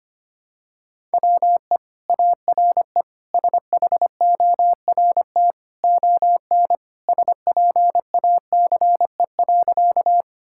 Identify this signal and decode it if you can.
Morse code, heard as a keyed tone, about 25 wpm, 700 Hz. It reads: WE ARE SHORT ON SPACE.
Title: Sentences from Top 1000 Words - Review Part 5 - 25wpm